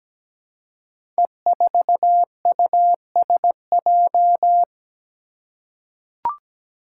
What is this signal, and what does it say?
Morse code, heard as a keyed tone, about 17 wpm, 700 Hz. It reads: E4USJ
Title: Five Challenging Letter-Number Combinations 17wpm